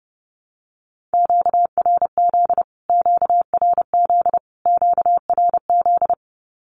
Morse code, heard as a keyed tone, about 30 wpm, 700 Hz. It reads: QRZ QRZ QRZ
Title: Q-Codes Repeated 3x 30wpm